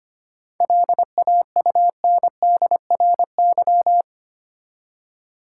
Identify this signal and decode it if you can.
Morse code, heard as a keyed tone, about 25 wpm, 700 Hz. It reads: LAUNDRY
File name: practice / 25wpm / Easy Words with Hints - 25wpm